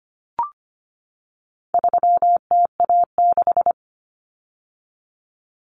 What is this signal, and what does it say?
Morse code, heard as a keyed tone, about 25 wpm, 700 Hz. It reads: 3TA6
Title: Common Cut Numbers - 4-Digit 25wpm